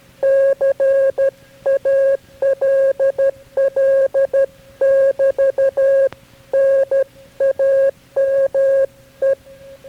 Morse code, heard as a keyed tone, about 13 wpm, 540 Hz. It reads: CALL <BT> NAME
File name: KB6MT-tape3-side1